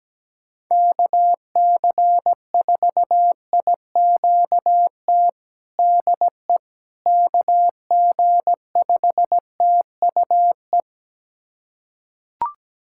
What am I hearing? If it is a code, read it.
Morse code, heard as a keyed tone, about 17 wpm, 700 Hz. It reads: KC4IQT DE KG5TUE
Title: USA Callsign DE Callsign 17wpm